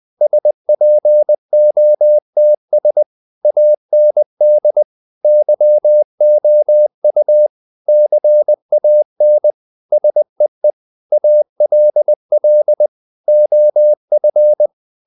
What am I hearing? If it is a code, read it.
Morse code, heard as a keyed tone, about 20 wpm, 600 Hz. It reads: SPOTS AND YOU CAN SEE ALL OF
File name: the old man and the sea part 2 20wpm